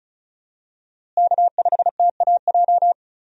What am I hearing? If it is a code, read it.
Morse code, heard as a keyed tone, about 36 wpm, 700 Hz. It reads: K5TAJ
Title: USA Call Signs - 5 Characters - Repeated 3x - 35wpm